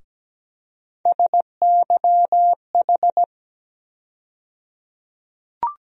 Morse code, heard as a keyed tone, about 17 wpm, 700 Hz. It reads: SYH
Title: USA Call Sign Postfixes - 3 Letters 17wpm